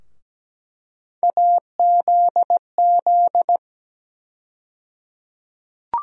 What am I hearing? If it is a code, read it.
Morse code, heard as a keyed tone, about 17 wpm, 700 Hz. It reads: AZZ
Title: USA Call Sign Postfixes - 3 Letters 17wpm